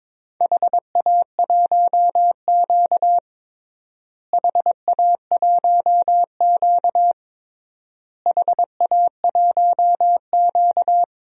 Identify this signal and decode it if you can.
Morse code, heard as a keyed tone, about 22 wpm, 700 Hz. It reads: HA1Q HA1Q HA1Q
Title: DXCC-100 Call Signs - 4 Characters - Repeated 3x - 22wpm